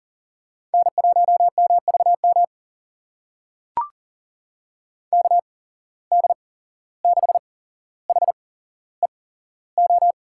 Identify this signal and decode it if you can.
Morse code, heard as a keyed tone, about 38 wpm, spaced as for 16 wpm, 700 Hz. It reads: N1MVM KD6HEO
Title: USA Call Signs - 8x Character Spacing - 40wpm